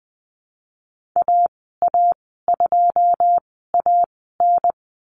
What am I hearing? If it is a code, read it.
Morse code, heard as a keyed tone, about 20 wpm, spaced as for 15 wpm, 700 Hz. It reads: AA2AN
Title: 231 Lesson 37g - Letter x - Call Signs 20wpm